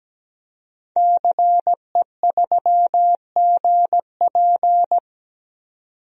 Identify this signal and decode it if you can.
Morse code, heard as a keyed tone, about 17 wpm, 700 Hz. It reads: CE3GP
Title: DXCC-100 Call Signs - 5 Characters - 17wpm